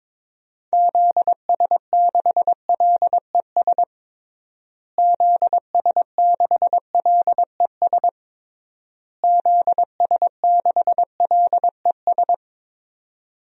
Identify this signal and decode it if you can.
Morse code, heard as a keyed tone, about 22 wpm, 700 Hz. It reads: ZS6LES ZS6LES ZS6LES
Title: DXCC-100 Call Signs - 6 Characters - Repeated 3x - 22wpm